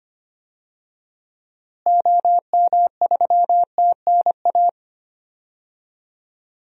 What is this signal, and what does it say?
Morse code, heard as a keyed tone, about 25 wpm, 700 Hz. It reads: OM3TNA
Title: DXCC-100 Call Signs - 6 Characters - Repeated 3x - 25wpm